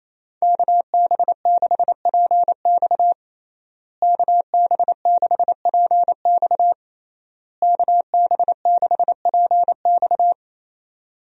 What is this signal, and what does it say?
Morse code, heard as a keyed tone, about 27 wpm, 700 Hz. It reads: KB6PX KB6PX KB6PX
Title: USA Call Signs - 5 Characters - Repeated 3x - 28wpm